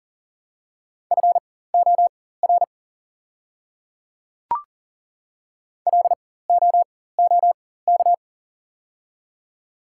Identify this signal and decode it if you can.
Morse code, heard as a keyed tone, about 40 wpm, spaced as for 19 wpm, 700 Hz. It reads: FOR LOOK
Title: 4x Character Spacing - Top 100 Words - 40wpm